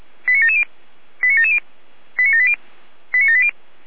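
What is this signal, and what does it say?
Five-tone selective call (ZVEI), heard as quick runs of five tones, tones of about 70 ms. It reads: 89770 89770 89780 89780
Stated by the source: ZVEI five-tone selective call received by SDR.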